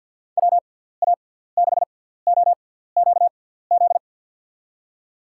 Morse code, heard as a keyed tone, about 47 wpm, spaced as for 17 wpm, 700 Hz. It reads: WA6OQZ
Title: USA Call Signs - 6x Character Spacing - 50wpm